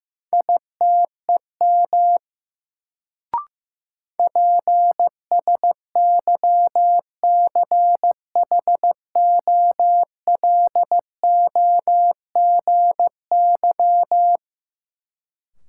Morse code, heard as a keyed tone, about 15 wpm, 700 Hz. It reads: ITEM PSYCHOLOGY
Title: Sentences from Top 1400 Words - New Word Drill Alt - 15wpm